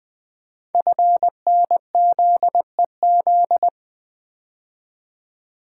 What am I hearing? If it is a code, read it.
Morse code, heard as a keyed tone, about 20 wpm, 700 Hz. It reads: FNZEZ
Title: Code Groups - 5 Letters - 20wpm